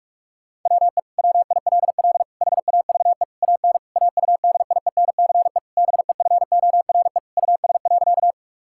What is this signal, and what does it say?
Morse code, heard as a keyed tone, about 42 wpm, 700 Hz. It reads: WE WILL HAVE AN AUDIENCE BEFORE US.
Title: Sentences from Top 1000 Words - Review Part 5 - 45wpm